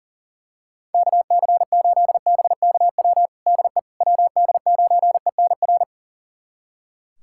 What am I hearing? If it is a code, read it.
Morse code, heard as a keyed tone, about 40 wpm, 690 Hz. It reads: KC8BKW DE WD9ENR
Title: USA Callsign DE Callsign - 6 Characters - 40wpm